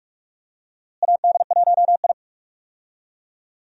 Morse code, heard as a keyed tone, about 48 wpm, 700 Hz. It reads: AD1I
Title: USA Call Signs - 4 Characters - Repeated 2x - 45wpm